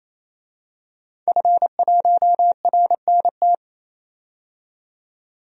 Morse code, heard as a keyed tone, about 28 wpm, 700 Hz. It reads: F1RNT
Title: DXCC-100 Call Signs - 5 Characters - 28wpm